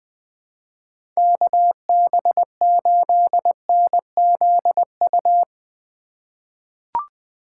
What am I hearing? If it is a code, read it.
Morse code, heard as a keyed tone, about 21 wpm, 700 Hz. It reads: KB8NZU
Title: USA Call Signs - 2x Character Spacing - 20wpm